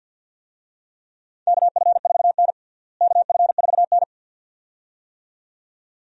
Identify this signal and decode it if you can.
Morse code, heard as a keyed tone, about 50 wpm, 700 Hz. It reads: KF4N KF4N
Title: USA Call Signs - 4 Characters - Repeated 2x - 50wpm